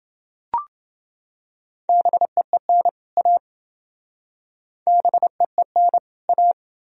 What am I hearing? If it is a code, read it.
Morse code, heard as a keyed tone, about 27 wpm, 700 Hz. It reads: BEEN A BEEN A